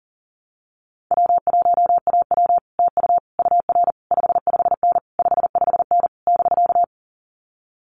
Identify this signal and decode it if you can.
Morse code, heard as a keyed tone, about 37 wpm, 700 Hz. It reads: W1AW TU UR 55N 55N <BK>
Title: Common QSO Phrases Encore 40wpm